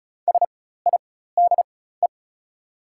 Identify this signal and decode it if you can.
Morse code, heard as a keyed tone, about 34 wpm, spaced as for 16 wpm, 690 Hz. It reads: SIDE
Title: 4x Character Spacing - Top 500 Words - 35wpm